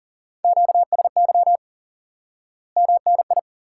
Morse code, heard as a keyed tone, about 40 wpm, 700 Hz. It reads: QSY MNI
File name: IQE Course 3 - Lesson 89 - YAGI - Two Elements - 40wpm